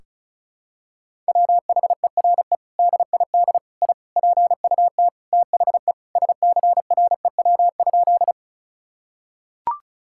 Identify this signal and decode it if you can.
Morse code, heard as a keyed tone, about 36 wpm, 700 Hz. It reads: WHERE DID I PUT THE SCREW?
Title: Sentences from Top 1600 Words 35wpm